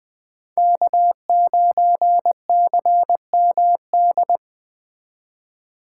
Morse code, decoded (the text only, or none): K9CMD